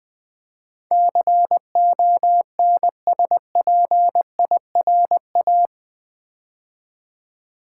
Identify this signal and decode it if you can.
Morse code, heard as a keyed tone, about 20 wpm, 700 Hz. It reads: CONSPIRA
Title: Build A Word - 4 Letters At A Time - 10-Letter Words - 20wpm